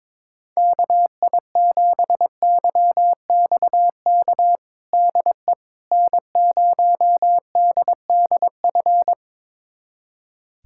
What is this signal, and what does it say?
Morse code, heard as a keyed tone, about 22 wpm, 700 Hz. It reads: KI7YXK DE N0DDF
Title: USA Callsign DE Callsign 22wpm